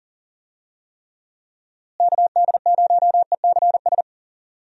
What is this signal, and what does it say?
Morse code, heard as a keyed tone, about 40 wpm, 700 Hz. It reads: KD0ECS
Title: USA Call Signs - 6 Characters - Repeated 3x - 40wpm